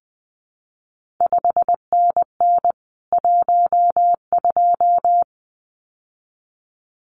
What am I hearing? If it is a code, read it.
Morse code, heard as a keyed tone, about 20 wpm, 700 Hz. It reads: 5NN 12
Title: CQ WW DX Contest Exchange Encore - 20wpm